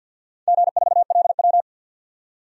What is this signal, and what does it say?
Morse code, heard as a keyed tone, about 51 wpm, 700 Hz. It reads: G4LW
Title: DXCC-100 Call Signs - 4 Characters - 50wpm